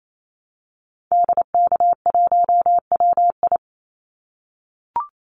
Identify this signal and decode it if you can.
Morse code, heard as a keyed tone, about 28 wpm, 700 Hz. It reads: DK1WI
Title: Call Signs - 28wpm